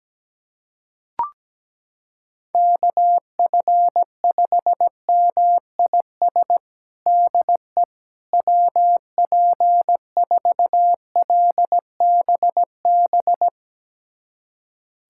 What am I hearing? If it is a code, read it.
Morse code, heard as a keyed tone, about 17 wpm, 700 Hz. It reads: KF5MIS DE WP4LBB
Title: USA Callsign DE Callsign - 6 Characters - 17wpm